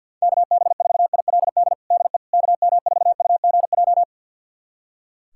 Morse code, heard as a keyed tone, about 45 wpm, 700 Hz. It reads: KB4ILD DE KM4UGJ
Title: USA Callsign DE Callsign - 6 Characters - 50wpm